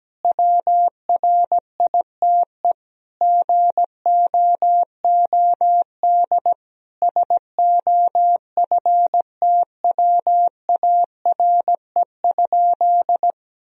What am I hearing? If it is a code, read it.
Morse code, heard as a keyed tone, about 17 wpm, 700 Hz. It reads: WRITE GOOD SOFTWARE?
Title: Sentences from Top 1000 Words - Review Part 3 - 17wpm